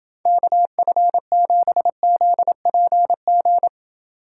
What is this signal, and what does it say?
Morse code, heard as a keyed tone, about 29 wpm, 700 Hz. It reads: KF7ZPG